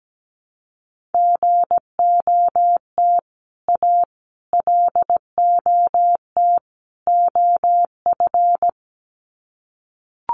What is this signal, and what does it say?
Morse code, heard as a keyed tone, about 17 wpm, 700 Hz. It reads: GOT A LOT OF
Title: Sets of 4 Words 17wpm